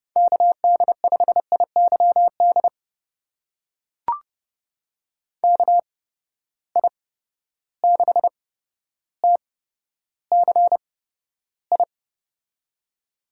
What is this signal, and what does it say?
Morse code, heard as a keyed tone, about 30 wpm, spaced as for 13 wpm, 700 Hz. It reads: KD5IYD KI6TCI
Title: USA Call Signs - 6 Characters - 8x Character Spacing - 30wpm